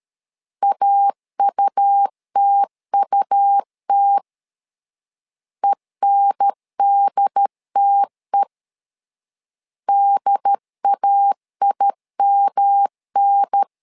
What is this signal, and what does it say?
Morse code, heard as a keyed tone, about 13 wpm, 800 Hz. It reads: AUTUT ENDTE DAIMN
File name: LEZIONI / 3 MWRO / 3 M 12wpm 1541